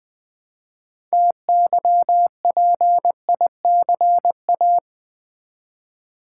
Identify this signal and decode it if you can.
Morse code, heard as a keyed tone, about 20 wpm, 700 Hz. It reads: TYPICA
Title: Build A Word - 3 Letters At A Time - 9-Letter Words - 20wpm